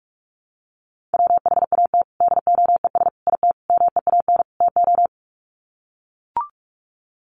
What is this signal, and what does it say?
Morse code, heard as a keyed tone, about 44 wpm, 700 Hz. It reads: WHAT DOES IT MEAN TO
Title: Sets of 5 Words Encore2 45wpm